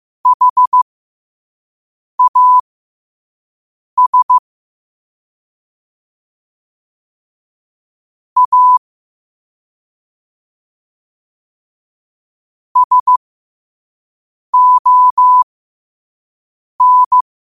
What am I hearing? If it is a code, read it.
Morse code, heard as a keyed tone, about 15 wpm, spaced as for 5 wpm, 1000 Hz. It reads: HAS A SON
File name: qotdfiles / 5wpm